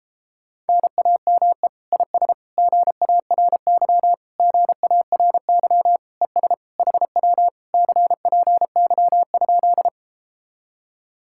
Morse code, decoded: NAME IS GARY GARY ES HW CPY?

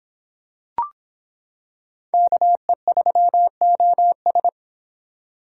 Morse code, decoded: KE3OS

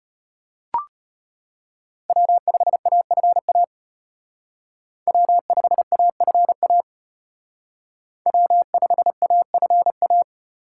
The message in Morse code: W5AFA W5AFA W5AFA

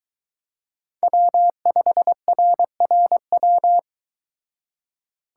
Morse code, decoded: W5RRW

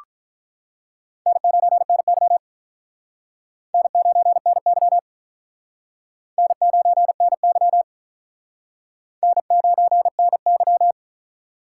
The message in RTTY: none

